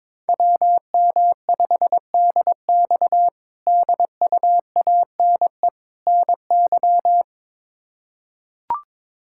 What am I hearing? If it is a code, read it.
Morse code, heard as a keyed tone, about 22 wpm, 700 Hz. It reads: WM5DX DUANE NY
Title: K1USN SST QSO Practice - 22wpm